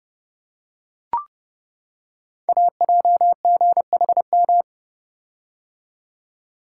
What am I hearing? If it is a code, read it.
Morse code, heard as a keyed tone, about 30 wpm, 700 Hz. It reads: AJGHM
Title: Code Groups - 5 Letters - 30wpm